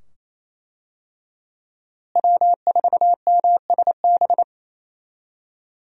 Morse code, heard as a keyed tone, about 28 wpm, 700 Hz. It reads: W4MSB
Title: USA Call Signs - 5 Characters - Repeated 2x - 28wpm